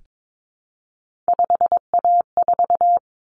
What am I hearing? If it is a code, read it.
Morse code, heard as a keyed tone, about 22 wpm, 700 Hz. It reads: 5A4
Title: Common Cut Numbers - 3-Digit 22wpm